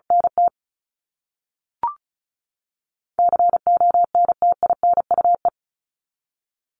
Morse code, heard as a keyed tone, about 34 wpm, 700 Hz. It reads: NT CONTINUE
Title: Top 751-1000 Words 35wpm